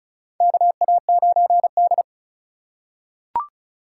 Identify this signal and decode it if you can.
Morse code, heard as a keyed tone, about 36 wpm, 700 Hz. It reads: KA9D
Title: USA Call Signs - 4 Characters - 3x Character Spacing - 35wpm